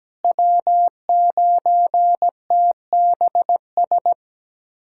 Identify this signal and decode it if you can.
Morse code, heard as a keyed tone, about 17 wpm, 700 Hz. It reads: W9TBS
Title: USA Callsign DE Callsign - 5 Characters - 17wpm